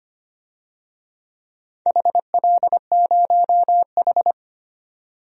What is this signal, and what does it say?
Morse code, heard as a keyed tone, about 25 wpm, 700 Hz. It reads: HL0H
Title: DXCC-100 Call Signs - 4 Characters - 25wpm